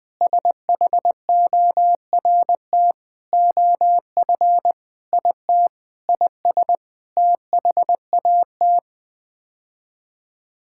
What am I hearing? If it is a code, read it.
Morse code, heard as a keyed tone, about 20 wpm, 700 Hz. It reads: SHORT OF IT IS THAT
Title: Sets of 9 Words - 20wpm